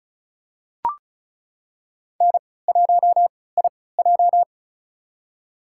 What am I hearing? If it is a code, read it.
Morse code, heard as a keyed tone, about 36 wpm, spaced as for 20 wpm, 700 Hz. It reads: N1IJ
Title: USA Call Signs - 4 Characters - 3x Character Spacing - 35wpm